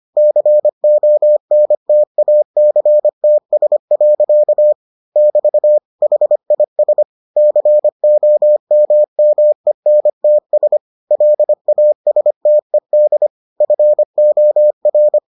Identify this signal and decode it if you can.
Morse code, heard as a keyed tone, about 25 wpm, 600 Hz. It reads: CONTACTS. <BT> HIS COMMENTS LASTED FOR